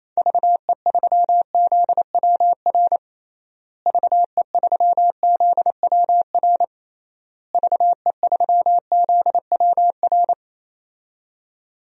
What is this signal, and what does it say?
Morse code, heard as a keyed tone, about 28 wpm, 700 Hz. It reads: VE3ZWR VE3ZWR VE3ZWR
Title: DXCC-100 Call Signs - 6 Characters - Repeated 3x - 28wpm